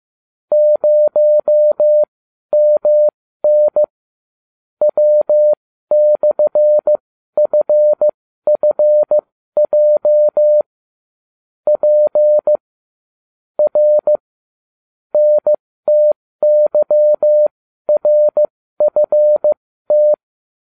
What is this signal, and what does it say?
Morse code, heard as a keyed tone, about 15 wpm, spaced as for 12 wpm, 600 Hz. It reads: 0MN W/FFJ P R NTYRFT